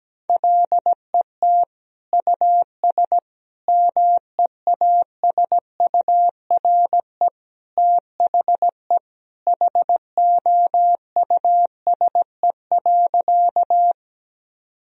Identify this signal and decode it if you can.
Morse code, heard as a keyed tone, about 17 wpm, 700 Hz. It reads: LET US MEASURE THE HOUSE.
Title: Sentences from Top 1000 Words - Review Part 2 - 17wpm